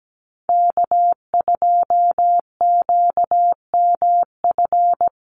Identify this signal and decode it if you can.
Morse code, heard as a keyed tone, about 17 wpm, 700 Hz. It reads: K2QMF
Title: Call Signs Repeated 2x - 17wpm